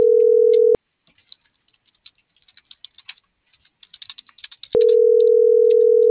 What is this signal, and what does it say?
Ringback tone (North America), 440 + 480 Hz.